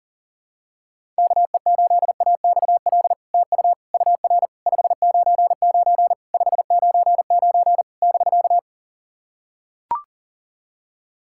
Morse code, heard as a keyed tone, about 42 wpm, 700 Hz. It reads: KE8AXL TU UR 599 599 <BK>
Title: SOTA QSO Practice 40wpm